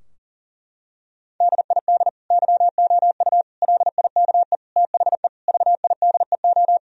Morse code, heard as a keyed tone, about 37 wpm, 700 Hz. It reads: DID YOU LIKE THE VIDEO
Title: Sentences from Top 1000 Words - Review Part 4 - 40wpm